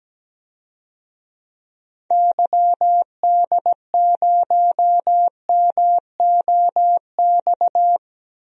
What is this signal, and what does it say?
Morse code, heard as a keyed tone, about 17 wpm, 700 Hz. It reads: YD0MOX
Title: DXCC-100 Call Signs - 6 Characters - 17wpm